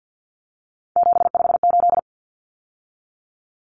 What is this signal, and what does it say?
Morse code, heard as a keyed tone, about 50 wpm, 700 Hz. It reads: 758
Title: Numbers - 3-Digit 50wpm